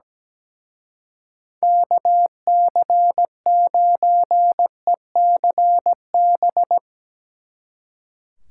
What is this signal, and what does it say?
Morse code, heard as a keyed tone, about 17 wpm, 700 Hz. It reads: KC9ECB
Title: Speed Racing - USA 2x3 Call Signs 17wpm